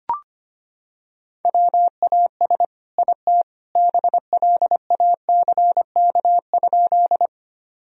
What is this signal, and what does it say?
Morse code, heard as a keyed tone, about 25 wpm, 700 Hz. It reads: WAS IT BLACK?
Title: Sentences from Top 1000 Words - Review Part 2 - 25wpm